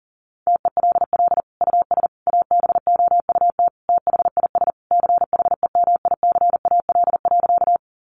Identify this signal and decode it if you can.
Morse code, heard as a keyed tone, about 40 wpm, 700 Hz. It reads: TELL US ABOUT THIS CHEMICAL.
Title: Sentences from Top 1200 Words 40wpm